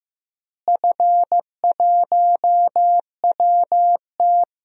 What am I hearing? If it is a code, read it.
Morse code, heard as a keyed tone, about 15 wpm, 700 Hz. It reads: F1WT
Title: DXCC-100 Call Signs - 4 Characters - Repeated 3x - 15wpm